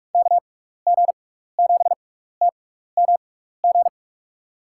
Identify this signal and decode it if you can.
Morse code, heard as a keyed tone, about 46 wpm, spaced as for 15 wpm, 700 Hz. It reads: KG7TMG
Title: USA Call Signs - 6x Character Spacing - 45wpm